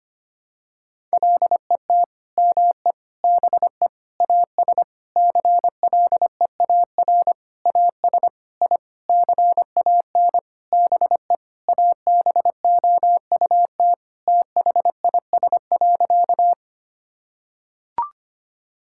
Morse code, heard as a keyed tone, about 25 wpm, 700 Hz. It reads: LET ME BE AS CLEAR AS I CAN BE ABOUT THIS.
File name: practice / 25wpm / Sentences with 9-Word Phrases - 25wpm